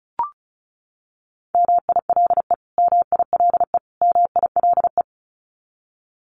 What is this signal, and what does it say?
Morse code, heard as a keyed tone, about 34 wpm, 700 Hz. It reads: MILE MILE MILE
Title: Top 500 Words Repeated 35wpm